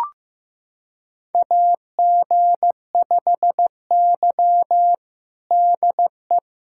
Morse code, heard as a keyed tone, about 15 wpm, 700 Hz. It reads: AG5Y DE
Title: USA Callsign DE Callsign - 4 Characters - 15wpm